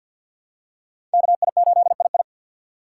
Morse code, heard as a keyed tone, about 52 wpm, 700 Hz. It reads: KI8II
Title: USA Call Signs - 5 Characters - 3x Character Spacing - 50wpm